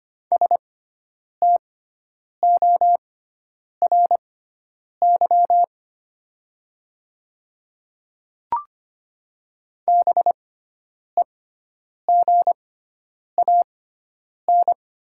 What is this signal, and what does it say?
Morse code, heard as a keyed tone, about 25 wpm, spaced as for 9 wpm, 700 Hz. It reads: STORY BEGAN
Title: 6x Character Spacing - Top 500 Words - 25wpm